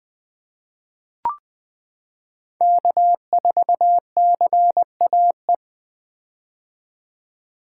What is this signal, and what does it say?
Morse code, heard as a keyed tone, about 20 wpm, 700 Hz. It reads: K4CAE
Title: 2023 POTA Hunter Call Signs - 20wpm